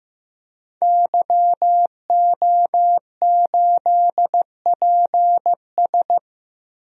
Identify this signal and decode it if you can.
Morse code, heard as a keyed tone, about 15 wpm, 700 Hz. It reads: YO8PS
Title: DXCC-100 Call Signs - 5 Characters - 15wpm